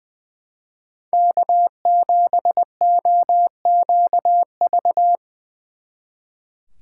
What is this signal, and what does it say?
Morse code, heard as a keyed tone, about 23 wpm, 700 Hz. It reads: K7OQV
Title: Speed Racing - USA 1x3 Call Signs 20wpm